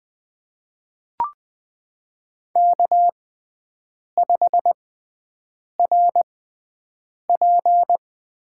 Morse code, heard as a keyed tone, about 20 wpm, spaced as for 7 wpm, 700 Hz. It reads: K5RP
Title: USA Call Signs - 5 Characters - 6x Character Spacing - 20wpm